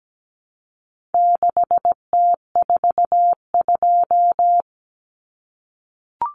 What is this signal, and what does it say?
Morse code, heard as a keyed tone, about 17 wpm, 700 Hz. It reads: 6T42
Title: Common Cut Numbers - 4-Digit 17wpm